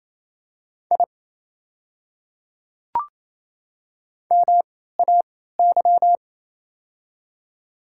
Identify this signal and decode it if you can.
Morse code, heard as a keyed tone, about 28 wpm, spaced as for 16 wpm, 700 Hz. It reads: I MAY
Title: 3x Character Spacing - Top 200 Words - 28wpm